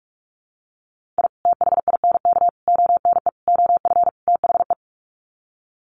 Morse code, heard as a keyed tone, about 44 wpm, 700 Hz. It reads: I THINK ONE OF THE
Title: Sets of 5 Words 45wpm